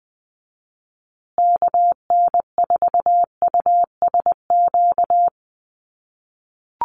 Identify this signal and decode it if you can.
Morse code, heard as a keyed tone, about 20 wpm, 700 Hz. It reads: KN4USQ
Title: USA Call Signs Repeated 3x 20wpm